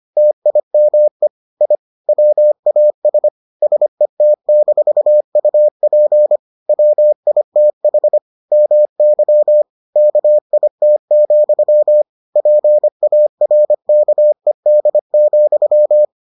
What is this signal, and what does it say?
Morse code, heard as a keyed tone, about 25 wpm, 600 Hz. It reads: TIME I WAS SET-UP WITH MY KIT, PARKED,